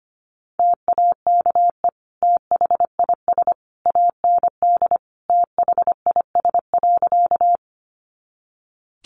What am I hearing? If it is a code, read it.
Morse code, heard as a keyed tone, about 25 wpm, 700 Hz. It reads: TAKE THIS AND THIS.
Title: Sentences from Top 200 Words 25wpm